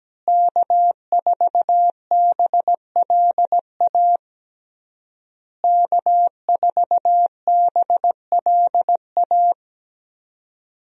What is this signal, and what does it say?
Morse code, heard as a keyed tone, about 17 wpm, 700 Hz. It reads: K4BLA K4BLA
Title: USA Call Signs - 5 Characters - Repeated 3x - 17wpm